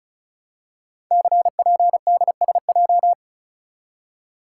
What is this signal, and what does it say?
Morse code, heard as a keyed tone, about 35 wpm, 700 Hz. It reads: CPDSJ
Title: Code Groups - 5 Letter:Numbers - 35wpm